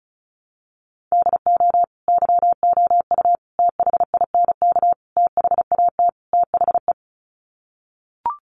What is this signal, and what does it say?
Morse code, heard as a keyed tone, about 34 wpm, 700 Hz. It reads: DO YOU THINK THAT THE
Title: Sets of 5 Words Encore 35wpm